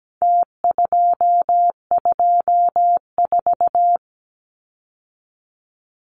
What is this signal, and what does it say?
Morse code, heard as a keyed tone, about 17 wpm, 700 Hz. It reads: T224
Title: Numbers - Contest Style - 4-Digit 17wpm